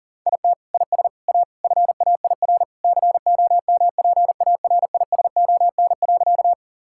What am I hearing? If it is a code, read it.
Morse code, heard as a keyed tone, about 40 wpm, 700 Hz. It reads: IT IS A FAIR COMPARISON.
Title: Sentences from Top 1600 Words 40wpm